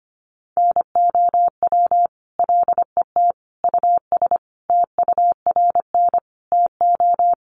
Morse code, heard as a keyed tone, about 25 wpm, 700 Hz. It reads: NOW LET US TURN TO
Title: Sentences from Top 600 Words 25wpm